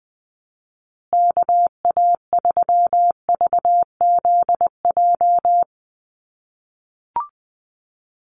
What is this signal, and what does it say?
Morse code, heard as a keyed tone, about 20 wpm, 700 Hz. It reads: KA3VZJ
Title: USA Call Signs Repeated 3x 20wpm